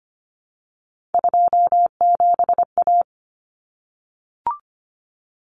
Morse code, heard as a keyed tone, about 25 wpm, 700 Hz. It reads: 27A
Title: Common Cut Numbers - 3-Digit 25wpm